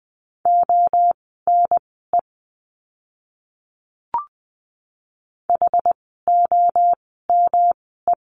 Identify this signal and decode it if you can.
Morse code, heard as a keyed tone, about 20 wpm, spaced as for 15 wpm, 700 Hz. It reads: ONE HOME
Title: 115 Lesson 19f - Letter w - Words 20wpm